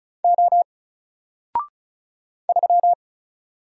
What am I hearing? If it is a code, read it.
Morse code, heard as a keyed tone, about 36 wpm, 700 Hz. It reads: O 3
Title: Single Letter-Number Alt 35wpm